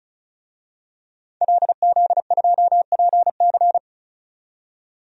Morse code, heard as a keyed tone, about 36 wpm, 700 Hz. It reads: LZ2PC
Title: DXCC-100 Call Signs - 5 Characters - 35wpm